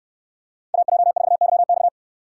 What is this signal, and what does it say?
Morse code, heard as a keyed tone, about 69 wpm, 700 Hz. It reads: AJ4YX